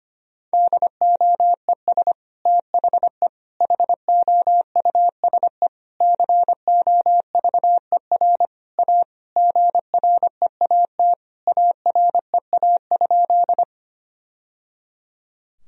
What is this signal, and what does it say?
Morse code, heard as a keyed tone, about 25 wpm, 700 Hz. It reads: DOES THE HOUSE COVER A GREAT AREA?
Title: Sentences from Top 1000 Words - Review Part 2 - 25wpm